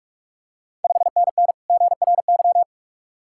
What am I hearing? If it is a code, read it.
Morse code, heard as a keyed tone, about 42 wpm, 700 Hz. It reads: 5NN GRY
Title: Ontario QSO Party Exchange 45wpm